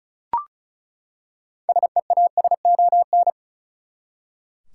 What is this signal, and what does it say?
Morse code, heard as a keyed tone, about 36 wpm, 700 Hz. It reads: SEASON
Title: Six Letter Words - 35wpm